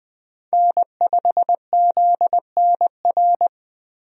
Morse code, heard as a keyed tone, about 21 wpm, 700 Hz. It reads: N5ZNR
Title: USA Call Signs - 5 Characters - 20wpm